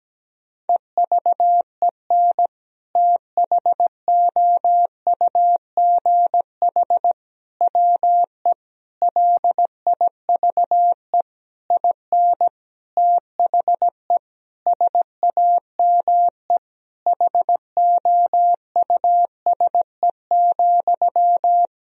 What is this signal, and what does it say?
Morse code, heard as a keyed tone, about 17 wpm, 700 Hz. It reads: EVEN THOUGH WE LIVE IN THE SAME HOUSE,
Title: Sentences from Top 1000 Words - Review Part 2 - 17wpm